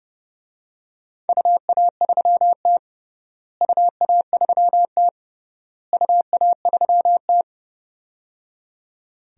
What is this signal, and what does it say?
Morse code, heard as a keyed tone, about 30 wpm, 700 Hz. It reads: UA3T UA3T UA3T
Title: DXCC-100 Call Signs - 4 Characters - Repeated 3x - 30wpm